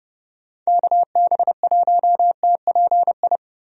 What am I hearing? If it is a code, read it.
Morse code, heard as a keyed tone, about 30 wpm, 700 Hz. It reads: KB1TPI
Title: USA Call Signs - 6 Characters - Repeated 3x - 30wpm